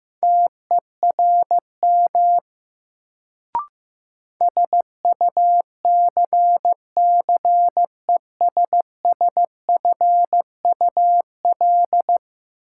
Morse code, heard as a keyed tone, about 15 wpm, 700 Hz. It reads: TERM SUCCESSFUL
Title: Sentences from Top 900 Words - New Word Drill - 15wpm